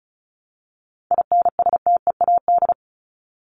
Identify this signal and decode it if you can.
Morse code, heard as a keyed tone, about 34 wpm, 700 Hz. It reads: INSTEAD
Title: Top 501-750 Words 35wpm